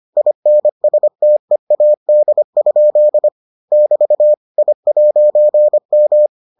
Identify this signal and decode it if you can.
Morse code, heard as a keyed tone, about 25 wpm, 600 Hz. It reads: INSTEAD? <BT> I'M